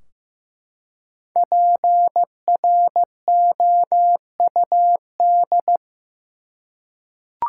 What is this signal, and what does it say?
Morse code, heard as a keyed tone, about 15 wpm, 700 Hz. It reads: PROUD
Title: Five Letter Words - 15wpm